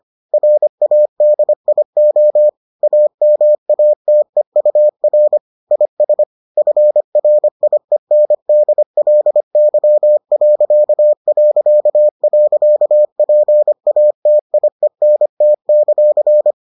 Morse code, heard as a keyed tone, about 25 wpm, 600 Hz. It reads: RADIO AMATEUR IS FRIENDLY...PATIENT;